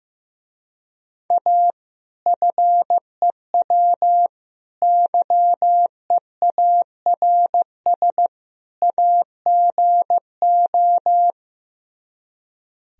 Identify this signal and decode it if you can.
Morse code, heard as a keyed tone, about 15 wpm, 700 Hz. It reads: A FEW YEARS AGO
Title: Speed-Racing - Sets of 4 Words 15wpm